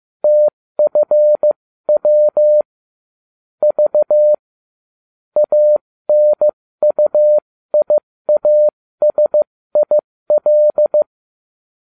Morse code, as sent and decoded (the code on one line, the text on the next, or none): - ..-. .-- / ...- / .- -. ..- .. .- ... .. .-..
TFW V ANUIASIL